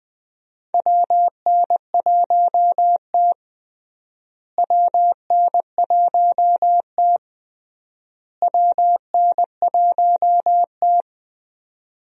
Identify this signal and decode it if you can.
Morse code, heard as a keyed tone, about 20 wpm, 700 Hz. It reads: WN1T WN1T WN1T
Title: USA Call Signs - 4 Characters - Repeated 3x - 20wpm